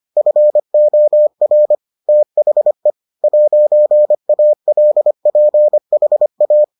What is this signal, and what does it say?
Morse code, heard as a keyed tone, about 25 wpm, 600 Hz. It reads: FOR THE 'ALPHA